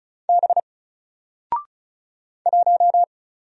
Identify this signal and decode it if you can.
Morse code, heard as a keyed tone, about 35 wpm, 700 Hz. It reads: B 1
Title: Single Letter-Number 35wpm